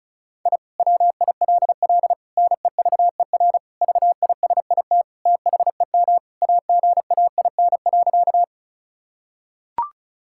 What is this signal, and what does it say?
Morse code, heard as a keyed tone, about 36 wpm, 700 Hz. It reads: I WILL NEVER VISIT THEM AGAIN.
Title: Sentences from Top 1000 Words - Review Part 6 - 35wpm